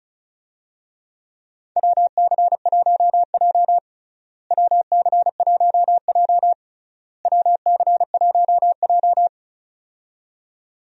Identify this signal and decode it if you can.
Morse code, heard as a keyed tone, about 36 wpm, 700 Hz. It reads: WC1J WC1J WC1J
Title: USA Call Signs - 4 Characters - Repeated 3x - 35wpm